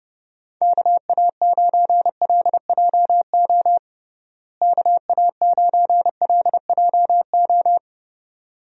Morse code, heard as a keyed tone, about 30 wpm, 700 Hz. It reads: KA9LJO KA9LJO
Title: USA Call Signs - 6 Characters - Repeated 2x - 30wpm